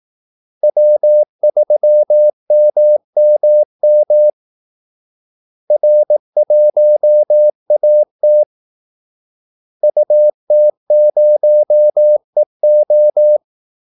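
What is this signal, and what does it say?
Morse code, heard as a keyed tone, about 18 wpm, 600 Hz. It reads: W3MMM R1AT UT0EO
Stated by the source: Morse call-sign practice at 18 wpm.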